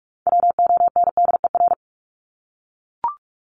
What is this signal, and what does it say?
Morse code, heard as a keyed tone, about 44 wpm, 700 Hz. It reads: WONDER
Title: Top 500 Words 45wpm